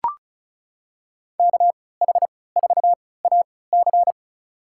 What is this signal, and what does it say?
Morse code, heard as a keyed tone, about 34 wpm, spaced as for 20 wpm, 700 Hz. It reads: KH4AC
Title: USA Call Signs - 3x Character Spacing - 35wpm